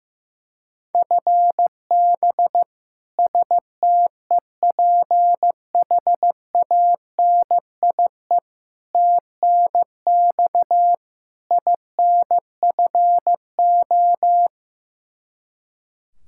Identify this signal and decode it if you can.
Morse code, heard as a keyed tone, about 15 wpm, 700 Hz. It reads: FB STEPHANIE TNX INFO
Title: Practice QSO - Exchange 3 - More Info - 15wpm